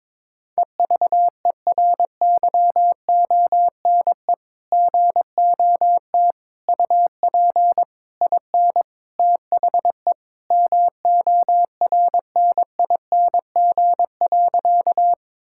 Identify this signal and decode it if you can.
Morse code, heard as a keyed tone, about 22 wpm, 700 Hz. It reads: EVERYONE GOT UP IN THE MORNING.
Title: Sentences from Top 1000 Words - Review Part 5 - 22wpm